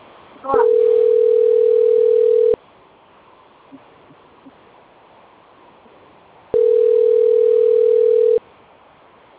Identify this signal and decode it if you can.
Ringback tone (North America), 440 + 480 Hz.